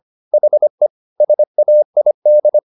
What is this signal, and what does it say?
Morse code, heard as a keyed tone, about 25 wpm, 600 Hz. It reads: HE SAID
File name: the old man and the sea part 1 25wpm